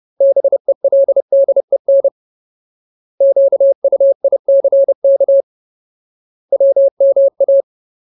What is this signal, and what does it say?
Morse code, heard as a keyed tone, about 30 wpm, 550 Hz. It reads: BELDEN QUICK WMA
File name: Ultimate Word List / 30WPM Ultimate